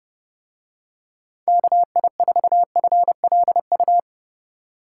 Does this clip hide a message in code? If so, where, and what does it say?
Yes: Morse code from 1.5 to 4.0 s, heard as a keyed tone, about 30 wpm, 700 Hz. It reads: KI4FLU